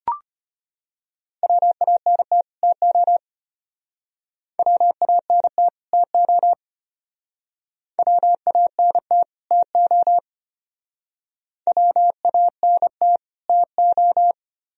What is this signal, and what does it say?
Morse code, heard as a keyed tone, about 33 wpm, 700 Hz. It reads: WANT TO WANT TO WANT TO WANT TO